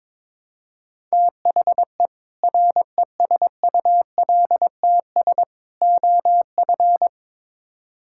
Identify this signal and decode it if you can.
Morse code, heard as a keyed tone, about 22 wpm, 700 Hz. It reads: THE RESULTS OF
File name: practice / 22wpm / Build A Phrase - 5-Word Phrases - 22wpm